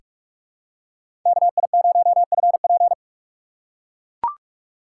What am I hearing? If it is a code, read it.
Morse code, heard as a keyed tone, about 43 wpm, 700 Hz. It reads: KI0FP
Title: USA Call Signs - 5 Characters - 4x Character Spacing - 45wpm